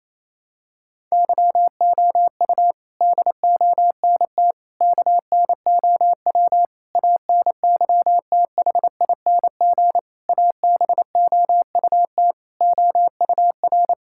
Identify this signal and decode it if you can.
Morse code, heard as a keyed tone, about 27 wpm, 700 Hz. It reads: YOU DONT KNOW ANYTHING ABOUT OUR
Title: Sentences from Top 1000 Words - Review Part 6 - 28wpm